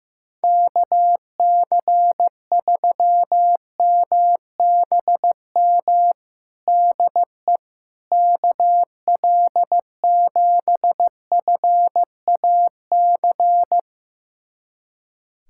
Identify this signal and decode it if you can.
Morse code, heard as a keyed tone, about 15 wpm, 700 Hz. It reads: KC3MBM DE KL7FAC
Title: USA Callsign DE Callsign 15wpm